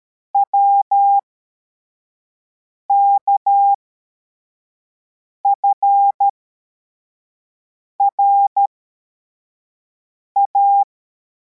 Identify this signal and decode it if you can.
Morse code, heard as a keyed tone, about 13 wpm, spaced as for 4 wpm, 800 Hz. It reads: WKFRA